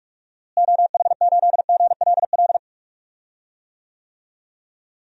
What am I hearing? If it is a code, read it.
Morse code, heard as a keyed tone, about 41 wpm, 700 Hz. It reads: OH8GLL
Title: DXCC-100 Call Signs - 6 Characters - Repeated 3x - 45wpm